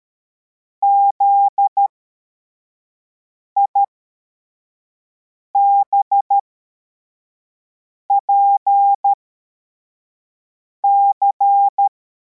Morse code, heard as a keyed tone, about 13 wpm, spaced as for 4 wpm, 800 Hz. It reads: ZIBPC
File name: formation / cw / Text71